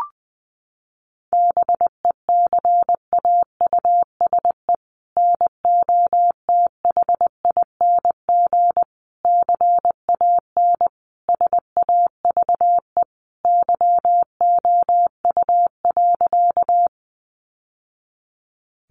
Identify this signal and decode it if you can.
Morse code, heard as a keyed tone, about 20 wpm, 700 Hz. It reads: BECAUSE NOTHING CAN SAVE YOU.
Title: Sentences from Top 1100 Words 20wpm